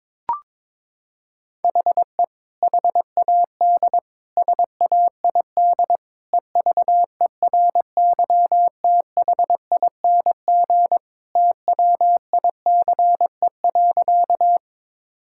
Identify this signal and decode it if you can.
Morse code, heard as a keyed tone, about 22 wpm, 700 Hz. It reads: HE HAD SAID EVERYTHING TWICE.
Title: Sentences from Top 1600 Words 22wpm